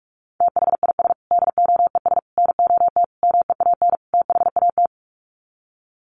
Morse code, heard as a keyed tone, about 44 wpm, 700 Hz. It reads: THIS DOES NOT MEAN THAT
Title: Sets of 5 Words Encore2 45wpm